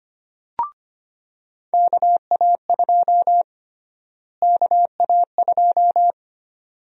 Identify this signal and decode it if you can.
Morse code, heard as a keyed tone, about 25 wpm, 700 Hz. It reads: KA2 KA2
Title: USA Call Sign Prefixes - Repeated 2x - 25wpm